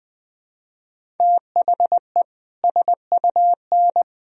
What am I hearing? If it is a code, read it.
Morse code, heard as a keyed tone, about 20 wpm, 700 Hz. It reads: THE SUN
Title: Sentences from Top 1000 Words - Review Part 1 - 20wpm